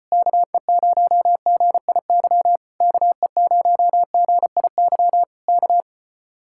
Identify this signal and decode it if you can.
Morse code, heard as a keyed tone, about 33 wpm, 700 Hz. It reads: KE0GIY KE0GIY K